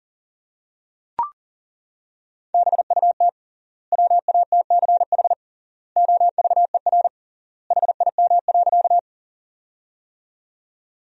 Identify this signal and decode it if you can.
Morse code, heard as a keyed tone, about 40 wpm, 700 Hz. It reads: BUT WATCH OVER HIM.
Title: 3x Word-Spacing - Sentences from Top 300 Words - 40wpm